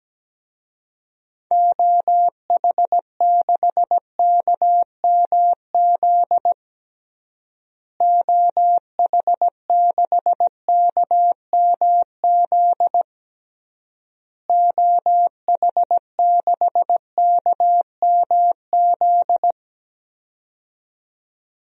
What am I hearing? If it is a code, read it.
Morse code, heard as a keyed tone, about 17 wpm, 700 Hz. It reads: OH6KMZ OH6KMZ OH6KMZ
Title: DXCC-100 Call Signs - 6 Characters - Repeated 3x - 17wpm